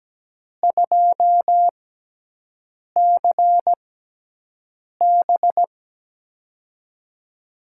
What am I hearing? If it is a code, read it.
Morse code, heard as a keyed tone, about 17 wpm, spaced as for 6 wpm, 700 Hz. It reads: 2CB
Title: USA Call Signs - 6x Character Spacing - 17wpm